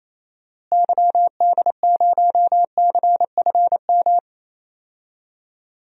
Morse code, heard as a keyed tone, about 28 wpm, 700 Hz. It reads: YD0CFM
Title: DXCC-100 Call Signs - 6 Characters - Repeated 3x - 28wpm